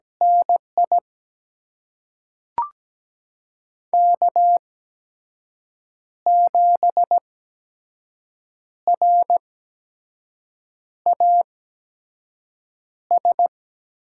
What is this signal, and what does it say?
Morse code, heard as a keyed tone, about 17 wpm, spaced as for 5 wpm, 700 Hz. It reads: NI K7RAS
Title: USA Call Signs - 8x Character Spacing - 17wpm